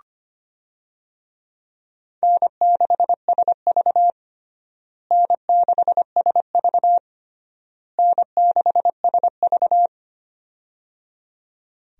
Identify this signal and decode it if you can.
Morse code, heard as a keyed tone, about 25 wpm, 700 Hz. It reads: N6SV N6SV N6SV
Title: USA Call Signs - 4 Characters - Repeated 3x - 25wpm